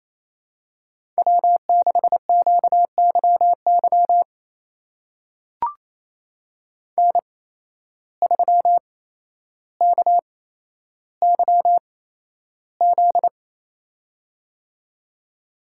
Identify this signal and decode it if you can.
Morse code, heard as a keyed tone, about 27 wpm, spaced as for 12 wpm, 700 Hz. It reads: W6QYY N3KYZ
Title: USA Call Signs - 5 Characters - 8x Character Spacing - 28wpm